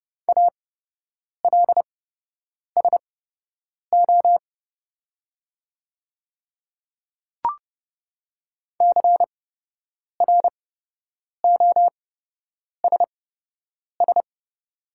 Morse code, heard as a keyed tone, about 29 wpm, spaced as for 8 wpm, 690 Hz. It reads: ALSO CROSS
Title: 8x Character Spacing - Top 300 Words - 30wpm